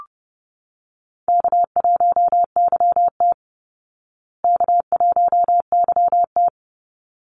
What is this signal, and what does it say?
Morse code, heard as a keyed tone, about 30 wpm, 700 Hz. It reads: K1YT K1YT
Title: Easy Call Signs Repeated 2x - 30wpm